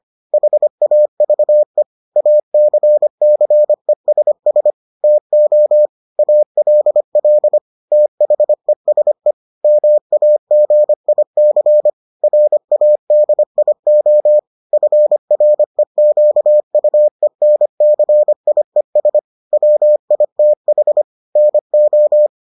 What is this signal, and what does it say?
Morse code, heard as a keyed tone, about 25 wpm, 600 Hz. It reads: HAVE ACCESS TO ALL THESE MAGIC RADIO FREQUENCIES WITH NO